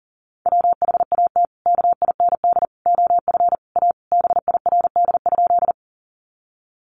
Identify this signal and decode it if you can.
Morse code, heard as a keyed tone, about 40 wpm, 700 Hz. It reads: WHAT KIND OF A BIRD?
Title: Sentences from Top 400 Words 40wpm